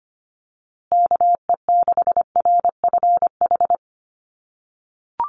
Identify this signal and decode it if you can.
Morse code, heard as a keyed tone, about 25 wpm, 700 Hz. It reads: KE6RFH
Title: USA Call Signs Repeated 3x 25wpm